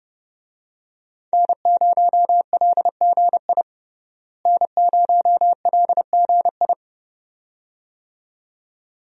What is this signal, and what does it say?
Morse code, heard as a keyed tone, about 30 wpm, 700 Hz. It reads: N0LGI N0LGI
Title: USA Call Signs - 5 Characters - Repeated 2x - 30wpm